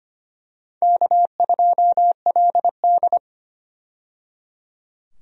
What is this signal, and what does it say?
Morse code, heard as a keyed tone, about 25 wpm, 700 Hz. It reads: K2LD
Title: USA Call Signs - 4 Characters - 25wpm